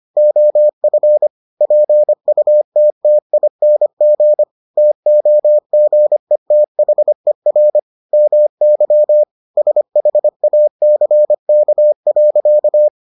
Morse code, heard as a keyed tone, about 25 wpm, 600 Hz. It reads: OF PUTTING TOGETHER MY SHACK.